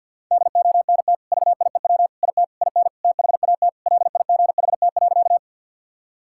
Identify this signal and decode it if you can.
Morse code, heard as a keyed tone, about 52 wpm, 700 Hz. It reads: DONT VIEW IT IN THAT LIGHT.
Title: Sentences from Top 1000 Words - Review Part 6 - 50wpm